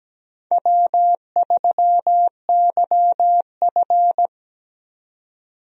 Morse code, heard as a keyed tone, about 17 wpm, 700 Hz. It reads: W3YF
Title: USA Call Signs - 4 Characters - 17wpm